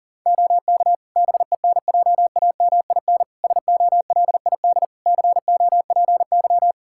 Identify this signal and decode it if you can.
Morse code, heard as a keyed tone, about 40 wpm, 700 Hz. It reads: OK BENJAMIN SOLID COPY
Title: K1USN SST QSO Practice 40wpm